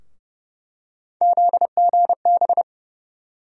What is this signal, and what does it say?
Morse code, heard as a keyed tone, about 31 wpm, 700 Hz. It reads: ZGB
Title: USA Call Sign Postfixes - 3 Letters 30wpm